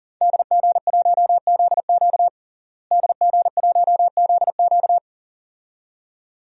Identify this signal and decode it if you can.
Morse code, heard as a keyed tone, about 40 wpm, 700 Hz. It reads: DG1ZQ DG1ZQ
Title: DXCC-100 Call Signs - 5 Characters - Repeated 2x - 40wpm